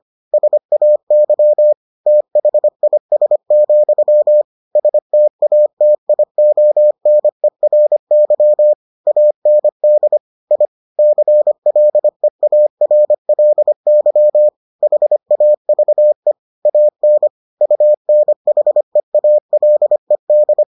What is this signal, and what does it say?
Morse code, heard as a keyed tone, about 25 wpm, 600 Hz. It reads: SAY THIS, STATIONERY AND I CLEARLY HAVE AN UNHEALED